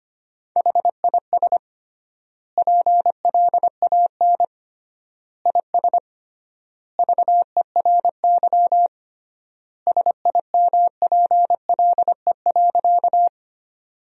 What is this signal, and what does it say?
Morse code, heard as a keyed tone, about 25 wpm, 700 Hz. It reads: HIS PLAN IS VERY SIMPLE.
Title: 3x Word-Spacing - Sentences from Top 500 Words - 25wpm